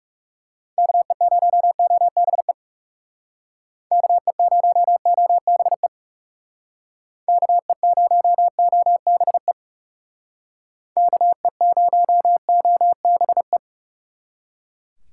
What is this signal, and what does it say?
Morse code, heard as a keyed tone, about 37 wpm, 700 Hz. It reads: KE0OBE KE0OBE KE0OBE KE0OBE
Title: Speed Racing - USA Call Signs 30wpm